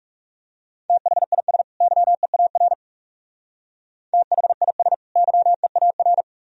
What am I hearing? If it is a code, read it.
Morse code, heard as a keyed tone, about 37 wpm, 700 Hz. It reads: THIS YEAR THIS YEAR
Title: Speed-Racing - Sets of 2 Words 30wpm